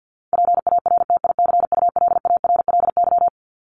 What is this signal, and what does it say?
Morse code, heard as a keyed tone, about 49 wpm, 700 Hz. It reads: PARTICULARLY
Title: LY Suffix - Top 100 Words with Base Words 50wpm